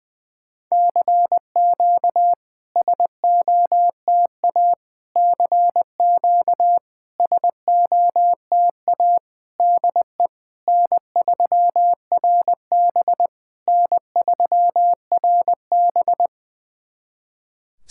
Morse code, heard as a keyed tone, about 21 wpm, 690 Hz. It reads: CQ SOTA CQ SOTA DE N3RB N3RB
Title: SOTA QSO Practice 20wpm